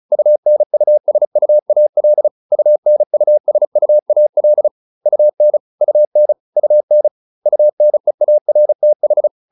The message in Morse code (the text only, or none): UNUSUAL UNUSUAL UN UN UN UNEARTH